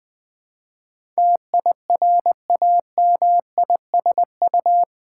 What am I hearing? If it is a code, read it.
Morse code, heard as a keyed tone, about 20 wpm, 690 Hz. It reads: TIRAMISU
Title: Difficult Words with Hints - 20wpm